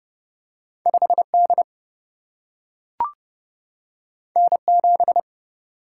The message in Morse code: 5D N7

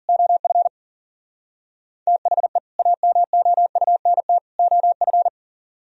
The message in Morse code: OF THE AMOUNT OF

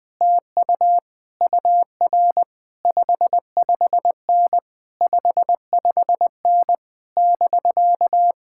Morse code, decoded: TU UR 55N 55N <BK>

